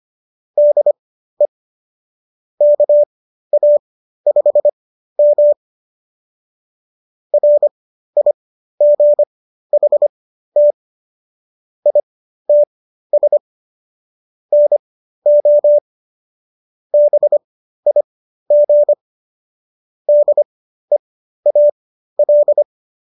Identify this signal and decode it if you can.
Morse code, heard as a keyed tone, about 25 wpm, spaced as for 13 wpm, 590 Hz. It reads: DE KA5M RIGHT ITS NO BIG DEAL